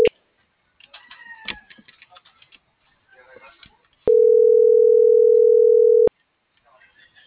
Ringback tone (North America), 440 + 480 Hz.